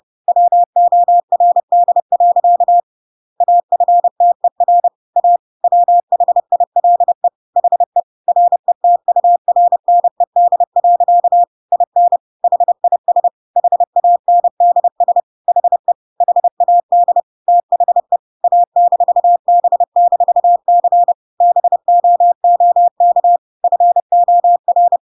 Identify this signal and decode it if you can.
Morse code, heard as a keyed tone, about 30 wpm, 700 Hz. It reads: WORD. AFTER A WHILE HE RETURNED. IN HIS HANDS HE HAD THE A-B-C BOOK FOR
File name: Pinocchio / Pinocchio ch8 30 30 ik1hge